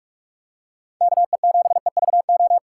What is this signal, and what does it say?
Morse code, heard as a keyed tone, about 45 wpm, 700 Hz. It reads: KE7EVO